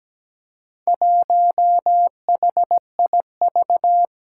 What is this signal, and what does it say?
Morse code, heard as a keyed tone, about 17 wpm, 700 Hz. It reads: 1HIV
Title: Four Challenging Letter-Number Combinations 17wpm